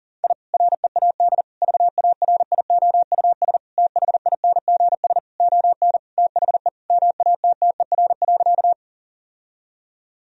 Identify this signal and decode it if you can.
Morse code, heard as a keyed tone, about 37 wpm, 700 Hz. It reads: I READ VARIOUS THINGS ON THE MATTER.
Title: Sentences from Top 1000 Words - Review Part 4 - 40wpm